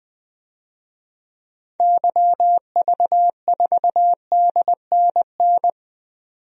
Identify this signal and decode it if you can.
Morse code, heard as a keyed tone, about 20 wpm, 700 Hz. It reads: YV4DNN
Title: DXCC-100 Call Signs - 6 Characters - 20wpm